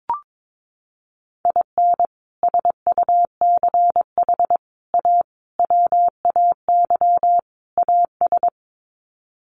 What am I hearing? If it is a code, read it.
Morse code, heard as a keyed tone, about 22 wpm, 700 Hz. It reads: IN SUCH A WAY AS
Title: Sets of 5 Words Encore 22wpm